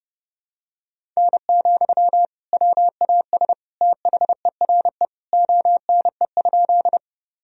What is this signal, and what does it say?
Morse code, heard as a keyed tone, about 32 wpm, 700 Hz. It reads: N, WAS THERE ONE?
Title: Sentences from Top 100 Words and Abbreviations 30wpm